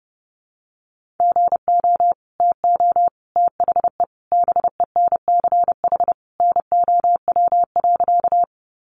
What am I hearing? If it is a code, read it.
Morse code, heard as a keyed tone, about 30 wpm, 700 Hz. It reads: GO TO THE BENCH NOW.
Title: Sentences from Top 1400 Words 30wpm